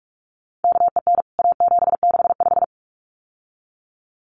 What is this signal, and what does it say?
Morse code, heard as a keyed tone, about 44 wpm, 700 Hz. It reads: KEN A765
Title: CWT Exchange 45wpm